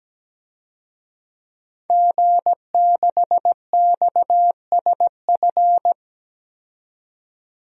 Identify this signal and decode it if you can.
Morse code, heard as a keyed tone, about 17 wpm, 700 Hz. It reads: G6XSF
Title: DXCC-100 Call Signs - 5 Characters - 17wpm